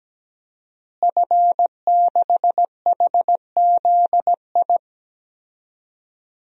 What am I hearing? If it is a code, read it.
Morse code, heard as a keyed tone, about 17 wpm, 700 Hz. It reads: F6HZI
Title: DXCC-100 Call Signs - 5 Characters - Repeated 3x - 17wpm